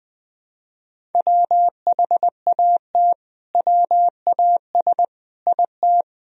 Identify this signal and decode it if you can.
Morse code, heard as a keyed tone, about 20 wpm, 700 Hz. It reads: WHAT WAS IT
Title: Sentences from Top 300 Words and Abbreviations 20wpm